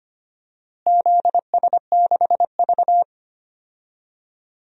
Morse code, heard as a keyed tone, about 25 wpm, 700 Hz. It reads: ZS6V
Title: DXCC-100 Call Signs - 4 Characters - Repeated 3x - 25wpm